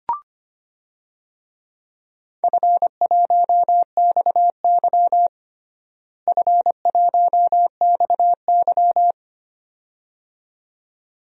Morse code, heard as a keyed tone, about 25 wpm, 700 Hz. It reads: F1XY F1XY
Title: DXCC-100 Call Signs - 4 Characters - Repeated 2x - 25wpm